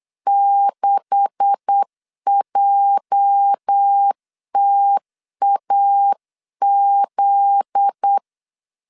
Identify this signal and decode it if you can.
Morse code, heard as a keyed tone, about 8 wpm, 800 Hz. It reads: 6JTAZ
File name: LEZIONI / 10 SIGNS / 10A SLASH 8wpm 3452